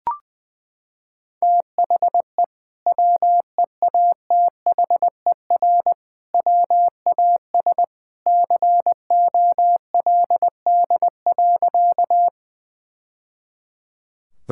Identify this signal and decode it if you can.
Morse code, heard as a keyed tone, about 20 wpm, 700 Hz. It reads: THE WEATHER WAS COLD.
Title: Sentences from Top 1000 Words - Review Part 5 - 20wpm